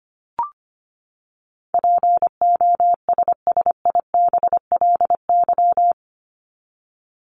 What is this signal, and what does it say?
Morse code, heard as a keyed tone, about 25 wpm, 700 Hz. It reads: POSSIBLY
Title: LY Suffix - Top 100 Words with Base Words 25wpm